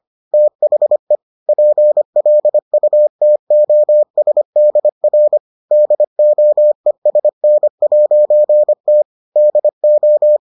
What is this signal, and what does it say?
Morse code, heard as a keyed tone, about 25 wpm, 600 Hz. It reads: THE PLUTOSDR DOESN'T DO